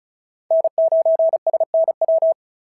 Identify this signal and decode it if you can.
Morse code, heard as a keyed tone, about 36 wpm, 690 Hz. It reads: N9SNW
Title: SOTA QSO Practice 35wpm